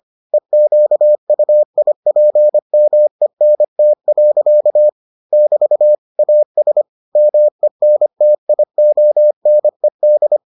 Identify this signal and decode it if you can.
Morse code, heard as a keyed tone, about 25 wpm, 600 Hz. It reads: EQUIPMENT. <BT> AS MENTIONED